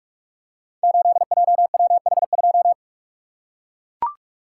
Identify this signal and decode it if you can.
Morse code, heard as a keyed tone, about 44 wpm, 700 Hz. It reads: 8JWH2
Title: Code Groups - 5 Letter:Numbers - 45wpm